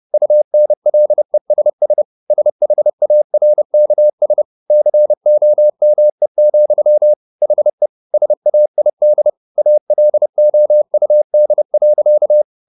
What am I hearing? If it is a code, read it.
Morse code, heard as a keyed tone, about 30 wpm, 600 Hz. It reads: UNLESS SHARKS COME, HE SAID ALOUD.
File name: the old man and the sea / the old man and the sea 1 30wpm